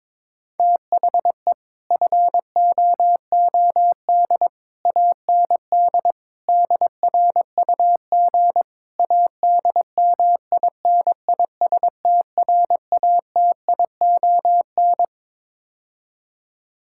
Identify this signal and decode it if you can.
Morse code, heard as a keyed tone, about 22 wpm, 700 Hz. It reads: THE FOOD AND DRUG ADMINISTRATION
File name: practice / 22wpm / Build A Phrase - 5-Word Phrases - 22wpm